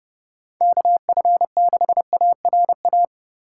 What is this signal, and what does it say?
Morse code, heard as a keyed tone, about 30 wpm, 700 Hz. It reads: KF6ARA
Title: USA Call Signs - 6 Characters - 3x Character Spacing - 30wpm